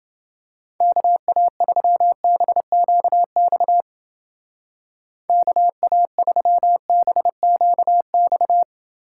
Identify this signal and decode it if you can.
Morse code, heard as a keyed tone, about 29 wpm, 700 Hz. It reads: KA3BQX KA3BQX